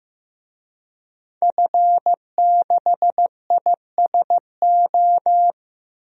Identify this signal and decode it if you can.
Morse code, heard as a keyed tone, about 15 wpm, 700 Hz. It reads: F6ISO
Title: DXCC-100 Call Signs - 5 Characters - Repeated 3x - 15wpm